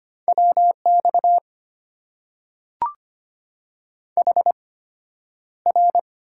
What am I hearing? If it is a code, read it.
Morse code, heard as a keyed tone, about 25 wpm, spaced as for 11 wpm, 700 Hz. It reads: WX HR
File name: Top 90 QSO Elements - 8x Character Spacing - 25wpm